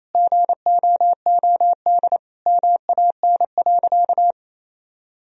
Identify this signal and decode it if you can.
Morse code, heard as a keyed tone, about 28 wpm, 700 Hz. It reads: GOOD MAN.
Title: Sentences from Top 1000 Words - Review Part 1 - 28wpm